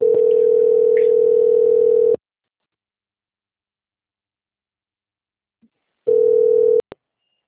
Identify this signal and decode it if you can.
Ringback tone (North America), 440 + 480 Hz.